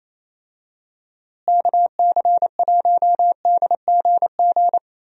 Morse code, heard as a keyed tone, about 28 wpm, 700 Hz. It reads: KC1DGG
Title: USA Call Signs - 6 Characters - Repeated 2x - 28wpm